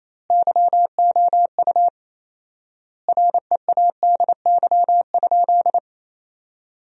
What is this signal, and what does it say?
Morse code, heard as a keyed tone, about 28 wpm, 700 Hz. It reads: YOU READY?
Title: 4x Word-Spacing - Sentences from Top 400 Words - 28wpm